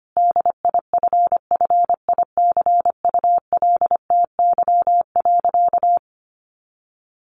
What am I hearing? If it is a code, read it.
Morse code, heard as a keyed tone, about 25 wpm, 700 Hz. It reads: DIFFICULTY.
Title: Sentences from Top 1500 Words 25wpm